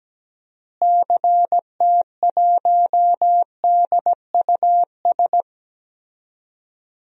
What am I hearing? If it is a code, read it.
Morse code, heard as a keyed tone, about 17 wpm, 700 Hz. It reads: CT1DUS
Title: DXCC-100 Call Signs - 6 Characters - Repeated 2x - 17wpm